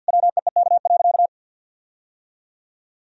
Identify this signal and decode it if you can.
Morse code, heard as a keyed tone, about 51 wpm, 700 Hz. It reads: WEEK.
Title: Sentences from Top 1000 Words - Review Part 4 - 50wpm